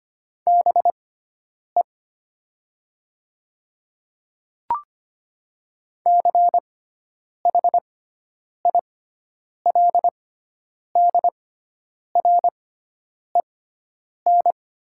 Morse code, heard as a keyed tone, about 25 wpm, spaced as for 9 wpm, 700 Hz. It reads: BE CHILDREN
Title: 6x Character Spacing - Top 500 Words - 25wpm